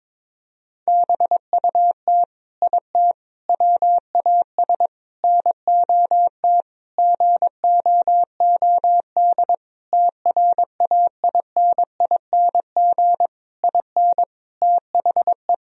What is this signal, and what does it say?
Morse code, heard as a keyed tone, about 22 wpm, 700 Hz. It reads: BUT IT WAS NOT GOOD TRAINING IN THE
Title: Sentences from Top 1000 Words - Review Part 4 - 22wpm